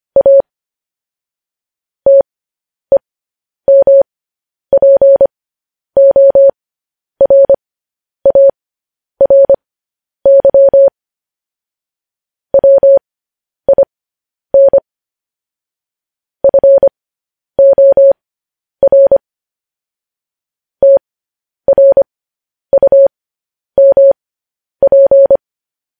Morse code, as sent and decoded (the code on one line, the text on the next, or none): .- / - . -- .--. --- .-. .- .-. -.-- / .-- .. -. / ..-. --- .-. / - .-. ..- -- .--.
A TEMPORARY WIN FOR TRUMP